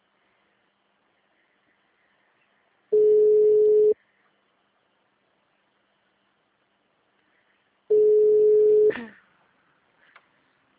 Ringback tone (Europe, 425 Hz).